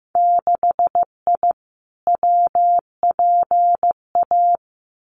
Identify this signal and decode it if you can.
Morse code, heard as a keyed tone, about 15 wpm, 700 Hz. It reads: 6I WPA
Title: Winter Field Day Exchanges Encore 15wpm